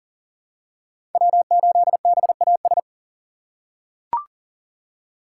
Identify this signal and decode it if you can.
Morse code, heard as a keyed tone, about 40 wpm, 700 Hz. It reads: W8BAS
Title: USA Call Signs - 5 Characters - 4x Character Spacing - 40wpm